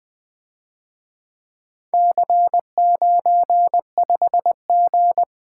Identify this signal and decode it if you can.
Morse code, heard as a keyed tone, about 24 wpm, 700 Hz. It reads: C95G
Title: Speed-Racing - Four Letter-Number Combinations - 20wpm